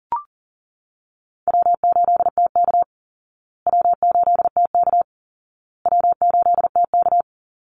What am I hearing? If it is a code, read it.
Morse code, heard as a keyed tone, about 40 wpm, 700 Hz. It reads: W8TK W8TK W8TK
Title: Easy Call Signs Repeated 3x - 40wpm